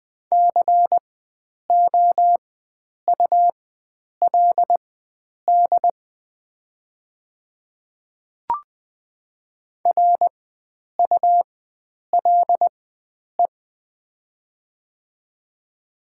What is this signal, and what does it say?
Morse code, heard as a keyed tone, about 20 wpm, spaced as for 9 wpm, 700 Hz. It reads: COULD RULE
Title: 4x Character Spacing - Top 500 Words - 20wpm